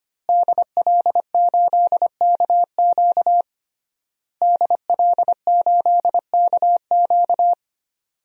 Morse code, heard as a keyed tone, about 25 wpm, 700 Hz. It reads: DL8KQ DL8KQ
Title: DXCC-100 Call Signs - 5 Characters - Repeated 3x - 25wpm